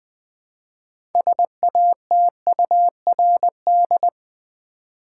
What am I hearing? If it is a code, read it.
Morse code, heard as a keyed tone, about 20 wpm, 700 Hz. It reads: SATURD
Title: Build A Word - 3 Letters At A Time - 8-Letter Words - 20wpm